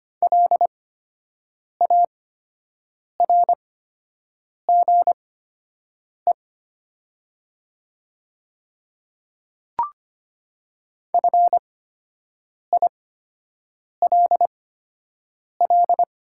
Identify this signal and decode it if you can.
Morse code, heard as a keyed tone, about 25 wpm, spaced as for 7 wpm, 700 Hz. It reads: LARGE FILL
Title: 8x Character Spacing - Top 500 Words - 25wpm